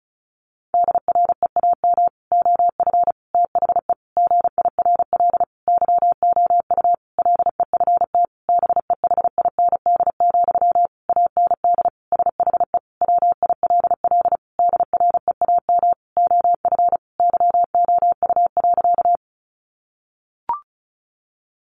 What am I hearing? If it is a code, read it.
Morse code, heard as a keyed tone, about 34 wpm, 700 Hz. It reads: DREAM OF THE GIRL YOU LEFT BEHIND, AND SHE WILL DREAM OF YOU.
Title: Sentences from Top 1500 Words 35wpm